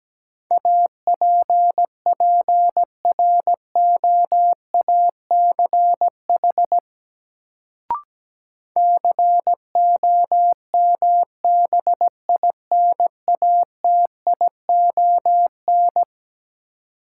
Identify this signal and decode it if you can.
Morse code, heard as a keyed tone, about 17 wpm, 700 Hz. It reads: APPROACH COMBINATION
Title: Sentences from Top 1000 Words - Review Part 6 - New Word Drill - 17wpm